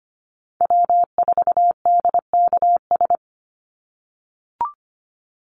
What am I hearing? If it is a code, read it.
Morse code, heard as a keyed tone, about 25 wpm, 700 Hz. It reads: W4DKS
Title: Call Signs - 25wpm